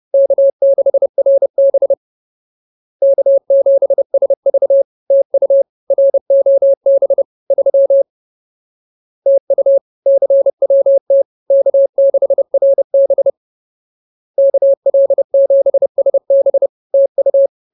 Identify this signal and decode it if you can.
Morse code, heard as a keyed tone, about 30 wpm, 550 Hz. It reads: K6RB K7SV TU ROB 3 TU CWT K6RB KL7SB TU